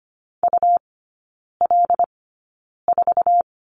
Morse code, heard as a keyed tone, about 25 wpm, spaced as for 9 wpm, 700 Hz. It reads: UL4
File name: Lesson 17e - Letter u - Three Characters 25wpm